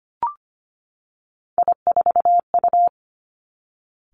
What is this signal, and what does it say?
Morse code, heard as a keyed tone, about 25 wpm, 700 Hz. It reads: I4U
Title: Three Challenging Letter-Number Combinations 25wpm